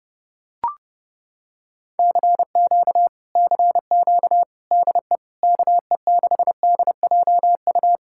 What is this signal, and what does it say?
Morse code, heard as a keyed tone, about 29 wpm, 700 Hz. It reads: CQ CQ DE KE6DJU